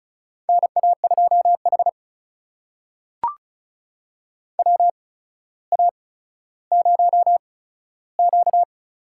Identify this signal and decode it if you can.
Morse code, heard as a keyed tone, about 36 wpm, spaced as for 15 wpm, 700 Hz. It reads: NA2H WA0Q
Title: USA Call Signs - 4 Characters - 8x Character Spacing - 35wpm